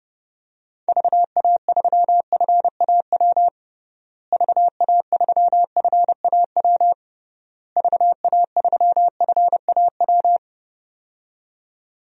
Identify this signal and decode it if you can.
Morse code, heard as a keyed tone, about 30 wpm, 700 Hz. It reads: VA3FAW VA3FAW VA3FAW
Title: DXCC-100 Call Signs - 6 Characters - Repeated 3x - 30wpm